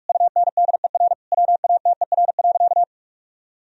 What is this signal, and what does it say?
Morse code, heard as a keyed tone, about 46 wpm, 700 Hz. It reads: UNDER WATER.
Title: Sentences from Top 1000 Words - Review Part 2 - 45wpm